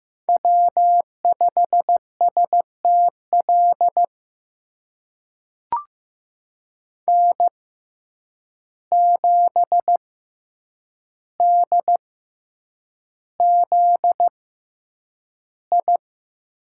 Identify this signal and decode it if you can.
Morse code, heard as a keyed tone, about 15 wpm, spaced as for 8 wpm, 700 Hz. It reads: W5STL N7DZI
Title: USA Call Signs - 5 Characters - 6x Character Spacing - 15wpm